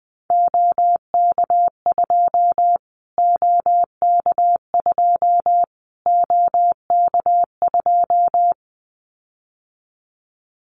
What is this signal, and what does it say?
Morse code, heard as a keyed tone, about 20 wpm, 700 Hz. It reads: OK2 OK2 OK2
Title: Callsign Prefixes for Top 25 DXCC - Repeated 3x - 20wpm